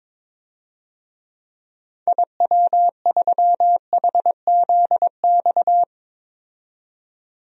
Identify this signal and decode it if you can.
Morse code, heard as a keyed tone, about 22 wpm, 700 Hz. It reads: IW3HZX
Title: DXCC-100 Call Signs - 6 Characters - 22wpm